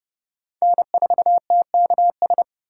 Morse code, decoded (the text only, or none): N4TKS